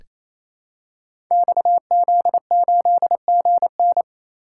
Morse code, heard as a keyed tone, about 28 wpm, 700 Hz. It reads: XZ8GN
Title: Code Groups - 5 Letter:Numbers - 28wpm